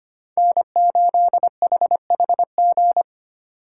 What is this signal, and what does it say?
Morse code, heard as a keyed tone, about 25 wpm, 700 Hz. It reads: N8HHG
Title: K1USN SST QSO Practice - 25wpm